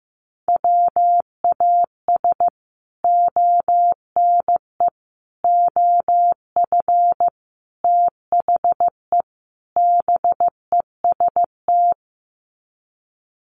Morse code, heard as a keyed tone, about 14 wpm, 700 Hz. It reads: WAS ONE OF THE BEST
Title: Sets of 5 Words Encore2 15wpm